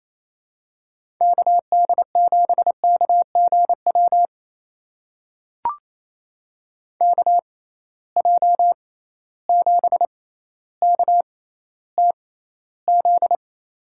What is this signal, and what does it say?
Morse code, heard as a keyed tone, about 28 wpm, spaced as for 14 wpm, 700 Hz. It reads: KD7KGW KJ7KTZ
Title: USA Call Signs - 6 Characters - 6x Character Spacing - 28wpm